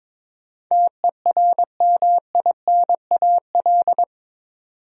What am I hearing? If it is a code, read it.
Morse code, heard as a keyed tone, about 22 wpm, 700 Hz. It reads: TERMINAL
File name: practice / 22wpm / Eight Letter Words - 22wpm